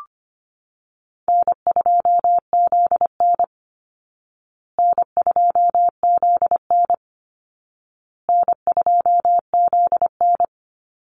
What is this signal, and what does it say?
Morse code, heard as a keyed tone, about 25 wpm, 700 Hz. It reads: N2ZN N2ZN N2ZN
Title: Call Signs Repeated 2x - 25wpm